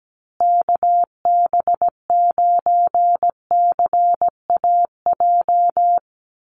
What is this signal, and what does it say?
Morse code, heard as a keyed tone, about 17 wpm, 700 Hz. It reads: KB9CAJ
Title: USA Call Signs Repeated 2x 17wpm